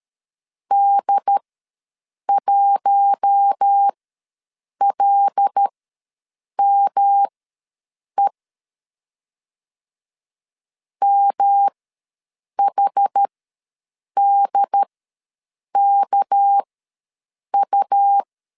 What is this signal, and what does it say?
Morse code, heard as a keyed tone, about 13 wpm, spaced as for 7 wpm, 800 Hz. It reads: D1LME MHDKU
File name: LEZIONI / 6 PKX0 / 6 PKX0 ws 12wpm 3448